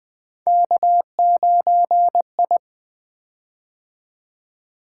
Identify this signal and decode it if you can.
Morse code, heard as a keyed tone, about 20 wpm, 700 Hz. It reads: K9I
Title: Three Letter-Number Combinations - 20wpm